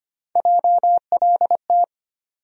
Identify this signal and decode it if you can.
Morse code, heard as a keyed tone, about 25 wpm, 700 Hz. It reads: JLT
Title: USA Call Sign Postfixes - 3 Letters 25wpm